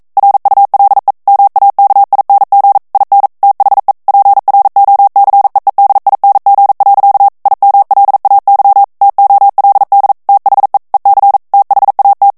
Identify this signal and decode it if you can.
Morse code, heard as a keyed tone, about 42 wpm, 800 Hz. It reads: RULE MAKING IN THE PROCEEDING. IMLAY TOLD THE EC THAT